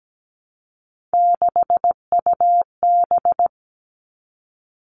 Morse code, heard as a keyed tone, about 17 wpm, 700 Hz. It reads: 6UB
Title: Three Challenging Letter-Number Combinations 17wpm